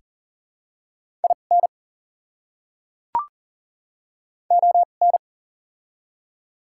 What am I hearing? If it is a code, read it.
Morse code, heard as a keyed tone, about 40 wpm, spaced as for 29 wpm, 700 Hz. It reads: IN ON
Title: 2x Character Spacing - Top 100 Words - 40wpm